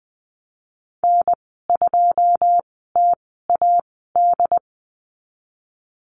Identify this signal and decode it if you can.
Morse code, heard as a keyed tone, about 21 wpm, spaced as for 15 wpm, 700 Hz. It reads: N2TAD
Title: 122 Lesson 20g - Number 3 - Call Signs 20wpm